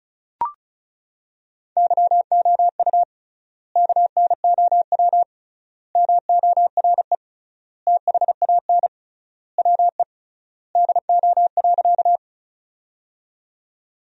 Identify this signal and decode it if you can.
Morse code, heard as a keyed tone, about 34 wpm, 700 Hz. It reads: YOU KNOW MORE THAN WE DO.
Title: 3x Word-Spacing - Sentences from Top 100 Words - 35wpm